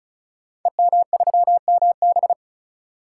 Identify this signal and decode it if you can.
Morse code, heard as a keyed tone, about 36 wpm, 700 Hz. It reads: EM3MB
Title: Code Groups - 5 Letter:Numbers - 35wpm